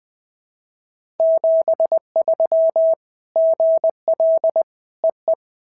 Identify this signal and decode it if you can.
Morse code, heard as a keyed tone, about 19 wpm, 650 Hz. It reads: 73 GL EE
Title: POTA QSO Practice - 20wpm